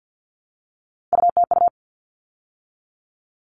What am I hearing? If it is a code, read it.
Morse code, heard as a keyed tone, about 50 wpm, 700 Hz. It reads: UTU
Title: All Cut Numbers - 3-Digit 50wpm